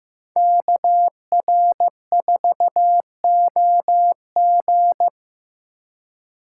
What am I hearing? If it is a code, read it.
Morse code, heard as a keyed tone, about 18 wpm, 700 Hz. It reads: KR4OG
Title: Speed Racing - USA Call Signs 15wpm